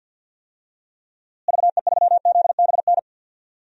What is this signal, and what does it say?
Morse code, heard as a keyed tone, about 53 wpm, 700 Hz. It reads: VE3ZBN
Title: DXCC-100 Call Signs - 6 Characters - Repeated 2x - 50wpm